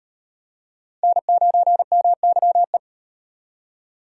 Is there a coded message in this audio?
Yes: Morse code from 1.0 to 2.8 s, heard as a keyed tone, about 37 wpm, 700 Hz. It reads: N9MYE